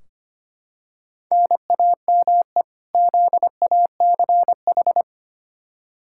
Morse code, heard as a keyed tone, about 25 wpm, 700 Hz. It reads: NAME ZACH
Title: K1USN SST QSO Practice 25wpm